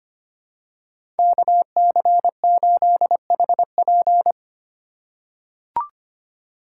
Morse code, heard as a keyed tone, about 25 wpm, 700 Hz. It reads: KC8HP
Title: USA Call Signs - 5 Characters - 2x Character Spacing - 25wpm